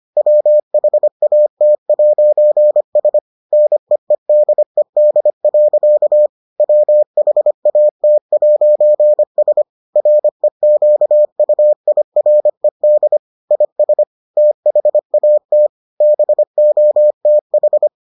Morse code, heard as a keyed tone, about 25 wpm, 600 Hz. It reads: WHAT'S NEEDED. WHAT'S REQUIRED IS THAT BOTH